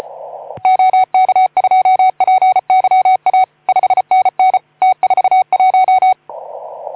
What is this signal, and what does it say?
Morse code, heard as a keyed tone, about 34 wpm, 750 Hz. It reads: OK2PYA 5NN T41
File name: Archiv / CUC181 / OK2ZC CUC 181